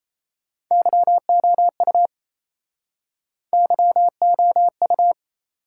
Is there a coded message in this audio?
Yes: Morse code from 0.7 to 5.1 s, heard as a keyed tone, about 35 wpm, 700 Hz. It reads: YOU YOU